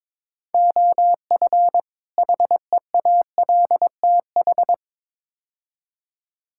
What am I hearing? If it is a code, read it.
Morse code, heard as a keyed tone, about 22 wpm, 700 Hz. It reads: OF HEALTH
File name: practice / 22wpm / Build A Phrase - 5-Word Phrases - 22wpm